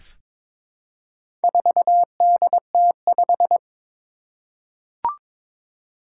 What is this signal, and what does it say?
Morse code, heard as a keyed tone, about 22 wpm, 700 Hz. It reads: 4DT5
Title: Four Challenging Letter-Number Combinations 22wpm